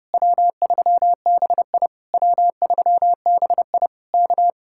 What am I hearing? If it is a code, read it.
Morse code, heard as a keyed tone, about 33 wpm, 700 Hz. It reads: W3BI W3BI K